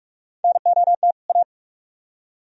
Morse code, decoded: NOT A

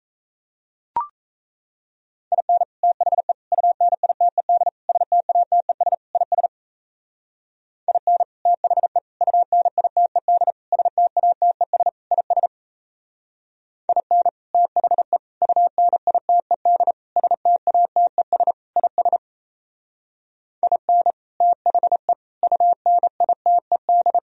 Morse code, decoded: IN THE UNITED STATES IS IN THE UNITED STATES IS IN THE UNITED STATES IS IN THE UNITED

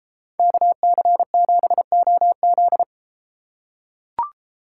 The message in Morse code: KC7OZ

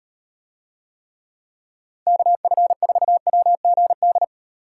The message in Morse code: KF4WGD

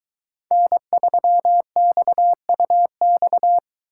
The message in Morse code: N3XUX

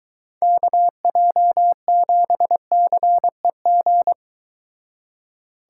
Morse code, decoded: KJ7CEG

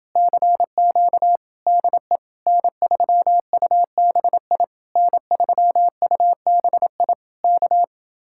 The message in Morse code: CQ DE N3UBI N3UBI K

